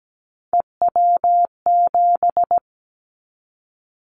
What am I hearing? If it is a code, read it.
Morse code, heard as a keyed tone, about 17 wpm, 700 Hz. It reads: EW7
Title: Callsign Prefixes for Top 100 DXCC - 17wpm